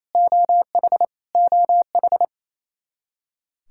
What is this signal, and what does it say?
Morse code, heard as a keyed tone, about 27 wpm, 700 Hz. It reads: OH OH
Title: K1USN SST QSO Practice 28wpm